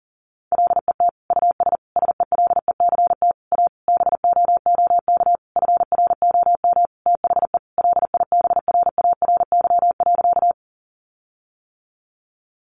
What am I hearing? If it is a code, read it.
Morse code, heard as a keyed tone, about 40 wpm, 700 Hz. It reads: LET US SELECT A BOOK FROM THE LIBRARY.
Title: Sentences from Top 1300 Words 40wpm